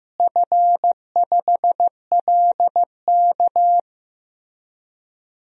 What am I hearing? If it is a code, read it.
Morse code, heard as a keyed tone, about 15 wpm, 700 Hz. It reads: F5LK
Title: DXCC-100 Call Signs - 4 Characters - Repeated 3x - 15wpm